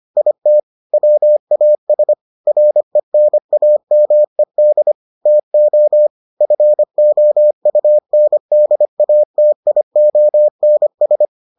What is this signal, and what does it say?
Morse code, heard as a keyed tone, about 25 wpm, 600 Hz. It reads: IT WAS RENAMED TO FOUNDATIONS